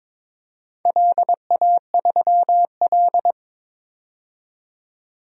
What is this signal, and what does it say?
Morse code, heard as a keyed tone, about 22 wpm, 700 Hz. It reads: LA3L
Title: DXCC-100 Call Signs - 4 Characters - 22wpm